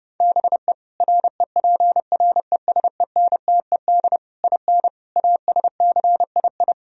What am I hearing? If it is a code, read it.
Morse code, heard as a keyed tone, about 33 wpm, 700 Hz. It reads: BE REPRESENTED IN ASCII